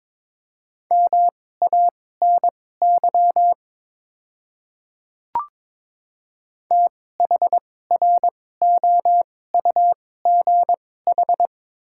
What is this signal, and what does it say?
Morse code, heard as a keyed tone, about 21 wpm, spaced as for 16 wpm, 700 Hz. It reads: MANY THROUGH
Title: 2x Character Spacing - Top 300 Words - 22wpm